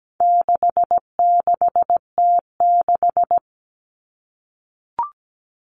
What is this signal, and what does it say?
Morse code, heard as a keyed tone, about 17 wpm, 700 Hz. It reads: 66T6
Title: Common Cut Numbers - 4-Digit 17wpm